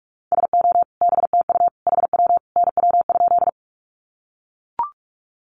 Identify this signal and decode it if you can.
Morse code, heard as a keyed tone, about 44 wpm, 700 Hz. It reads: SO BTU HW NW?
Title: Common QSO Phrases 45wpm